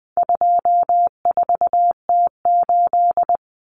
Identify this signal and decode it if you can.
Morse code, heard as a keyed tone, about 20 wpm, 700 Hz. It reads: 24T8
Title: Common Cut Numbers - 4-Digit 20wpm